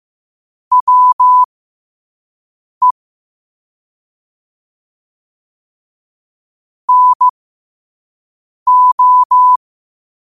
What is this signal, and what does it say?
Morse code, heard as a keyed tone, about 15 wpm, spaced as for 5 wpm, 1000 Hz. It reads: WE NO
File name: qotdfiles / 5wpm